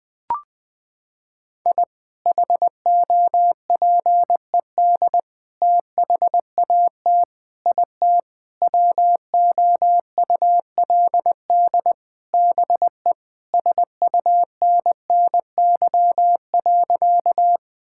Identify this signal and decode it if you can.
Morse code, heard as a keyed tone, about 20 wpm, 700 Hz. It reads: I HOPED THAT IT WOULD BE SUNNY.
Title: Sentences with 4-Word Phrases - 20wpm